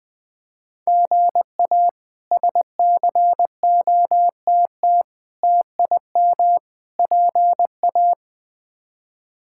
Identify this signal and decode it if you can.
Morse code, heard as a keyed tone, about 21 wpm, 700 Hz. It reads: GA SCOTT TIM PA
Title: K1USN SST QSO Practice - 20wpm